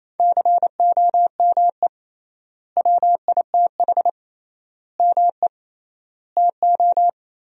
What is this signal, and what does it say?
Morse code, heard as a keyed tone, about 28 wpm, 700 Hz. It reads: COME WITH ME TO
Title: 3x Word-Spacing - Sentences from Top 200 Words - 28wpm